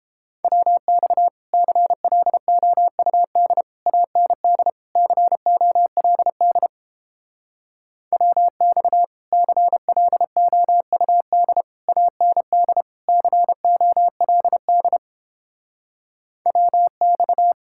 Morse code, heard as a keyed tone, about 31 wpm, 700 Hz. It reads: WX CLOUD AND COLD WX CLOUD AND COLD WX